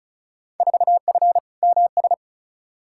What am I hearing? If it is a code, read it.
Morse code, heard as a keyed tone, about 34 wpm, 700 Hz. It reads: 4F MS
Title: Field Day Exchanges Encore - 35wpm